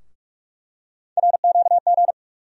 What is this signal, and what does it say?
Morse code, heard as a keyed tone, about 42 wpm, 700 Hz. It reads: RQG
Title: USA Call Sign Postfixes - 3 Letters 45wpm